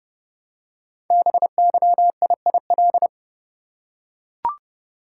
Morse code, heard as a keyed tone, about 30 wpm, 700 Hz. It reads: BYIIL
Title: Code Groups - 5 Letters - 30wpm